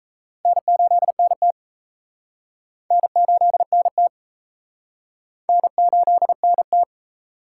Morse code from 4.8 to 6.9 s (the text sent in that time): N8NT